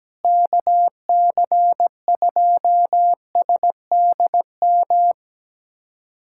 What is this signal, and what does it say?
Morse code, heard as a keyed tone, about 17 wpm, 700 Hz. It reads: KC2SDM
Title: USA Callsign DE Callsign - 6 Characters - 17wpm